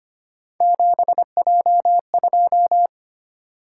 Morse code, heard as a keyed tone, about 25 wpm, 700 Hz. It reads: 7J2
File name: Three Letter-Number Combinations - 25wpm